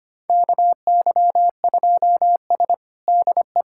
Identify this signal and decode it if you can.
Morse code, heard as a keyed tone, about 25 wpm, 700 Hz. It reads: KY2S DE
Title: USA Callsign DE Callsign - 4 Characters - 25wpm